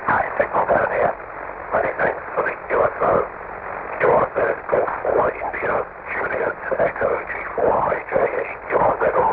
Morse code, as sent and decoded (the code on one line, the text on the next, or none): none